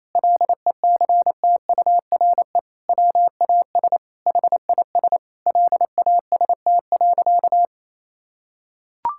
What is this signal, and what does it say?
Morse code, heard as a keyed tone, about 28 wpm, 700 Hz. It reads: LECTURE WAS HIS LAST.
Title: Sentences from Top 1600 Words 28wpm